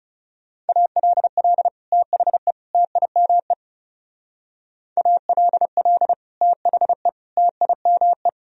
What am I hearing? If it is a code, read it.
Morse code, heard as a keyed tone, about 37 wpm, 700 Hz. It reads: ALL THE TIME ALL THE TIME
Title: Speed-Racing - Sets of 3 Words 30wpm